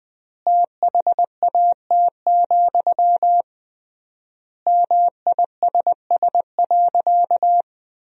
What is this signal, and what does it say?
Morse code, heard as a keyed tone, about 21 wpm, 700 Hz. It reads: THAT, MISS.
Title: 3x Word-Spacing - Sentences from Top 500 Words - 20wpm